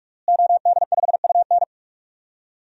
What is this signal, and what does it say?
Morse code, heard as a keyed tone, about 41 wpm, 700 Hz. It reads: OD5UN
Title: DXCC-100 Call Signs - 5 Characters - Repeated 3x - 45wpm